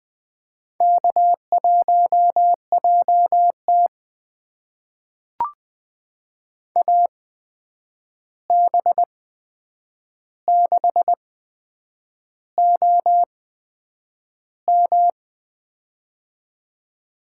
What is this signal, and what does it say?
Morse code, heard as a keyed tone, about 20 wpm, spaced as for 8 wpm, 700 Hz. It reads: K1JT AB6OM
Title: USA Call Signs - 8x Character Spacing - 20wpm